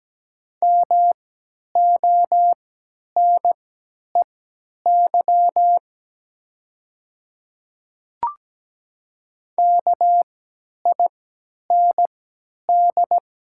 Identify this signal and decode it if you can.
Morse code, heard as a keyed tone, about 17 wpm, spaced as for 10 wpm, 700 Hz. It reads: MONEY KIND
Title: 3x Character Spacing - Top 500 Words - 17wpm